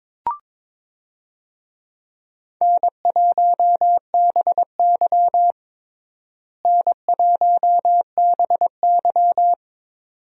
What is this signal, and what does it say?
Morse code, heard as a keyed tone, about 22 wpm, 700 Hz. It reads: N1BY N1BY
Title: USA Call Signs - 4 Characters - Repeated 2x - 22wpm